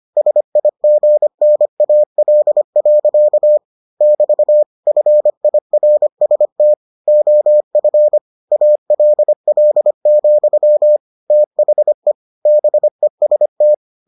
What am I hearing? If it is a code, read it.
Morse code, heard as a keyed tone, about 25 wpm, 600 Hz. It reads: SIGNAL. <BT> FIRST OF ALL, THE BEST